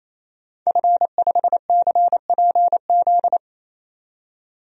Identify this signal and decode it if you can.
Morse code, heard as a keyed tone, about 27 wpm, 700 Hz. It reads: F5CPZ
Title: DXCC-100 Call Signs - 5 Characters - 28wpm